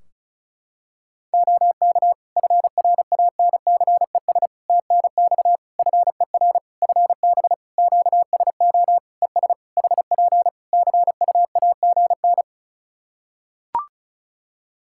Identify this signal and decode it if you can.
Morse code, heard as a keyed tone, about 36 wpm, 700 Hz. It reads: OK FRANCES TNX FER FB QSO ES HP CUAGN
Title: Practice QSO - Exchange 4 - Ending - 35wpm